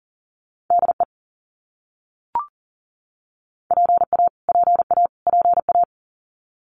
Morse code, heard as a keyed tone, about 40 wpm, 700 Hz. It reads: DE PA PA PA
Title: US States Repeated 3x 40wpm